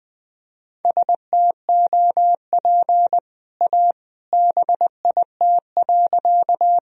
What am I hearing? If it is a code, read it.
Morse code, heard as a keyed tone, about 20 wpm, 700 Hz. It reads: STOP A BIT.
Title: Sentences from Top 1000 Words - Review Part 1 - 20wpm